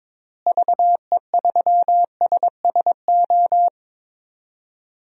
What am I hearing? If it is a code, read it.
Morse code, heard as a keyed tone, about 22 wpm, 700 Hz. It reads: VE3SSO
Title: DXCC-100 Call Signs - 6 Characters - Repeated 3x - 22wpm